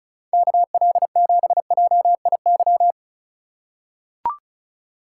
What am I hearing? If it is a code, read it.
Morse code, heard as a keyed tone, about 34 wpm, 700 Hz. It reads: KL7JIY
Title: USA Call Signs - 6 Characters - 2x Character Spacing - 35wpm